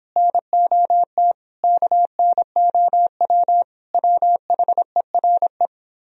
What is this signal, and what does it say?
Morse code, heard as a keyed tone, about 24 wpm, 700 Hz. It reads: NOT KNOW WHERE